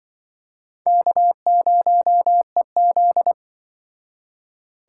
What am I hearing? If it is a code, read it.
Morse code, heard as a keyed tone, about 25 wpm, 700 Hz. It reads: K0EZ